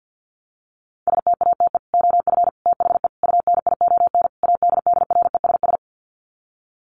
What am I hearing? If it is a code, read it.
Morse code, heard as a keyed tone, about 50 wpm, 700 Hz. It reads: STATE OF THE UNION ADDRESS
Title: Sets of 5 Words Encore 50wpm